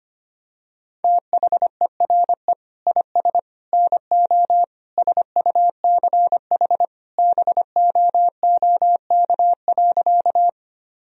Morse code, heard as a keyed tone, about 25 wpm, 700 Hz. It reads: THERE IS NO SUCH BOOK.
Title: Sentences from Top 1000 Words - Review Part 1 - 25wpm